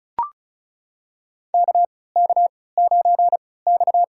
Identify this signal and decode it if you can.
Morse code, heard as a keyed tone, about 36 wpm, spaced as for 20 wpm, 700 Hz. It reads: KK9X
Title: USA Call Signs - 4 Characters - 3x Character Spacing - 35wpm